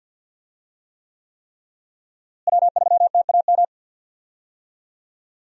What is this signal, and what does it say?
Morse code, heard as a keyed tone, about 52 wpm, 700 Hz. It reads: W3TAM
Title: USA Call Signs - 5 Characters - Repeated 3x - 50wpm